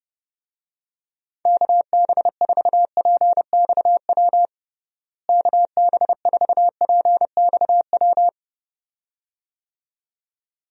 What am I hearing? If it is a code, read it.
Morse code, heard as a keyed tone, about 30 wpm, 700 Hz. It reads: KB4PXW KB4PXW
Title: USA Call Signs - 6 Characters - Repeated 2x - 30wpm